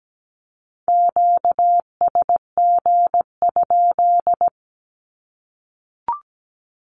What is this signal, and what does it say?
Morse code, heard as a keyed tone, about 17 wpm, 700 Hz. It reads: QSG?
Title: Q-Codes 17wpm